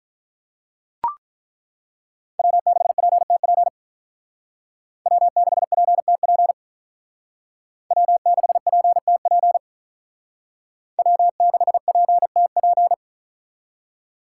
Morse code, heard as a keyed tone, about 42 wpm, 700 Hz. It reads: W6PTP W6PTP W6PTP W6PTP
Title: Speed Racing - USA 1x3 Call Signs 35wpm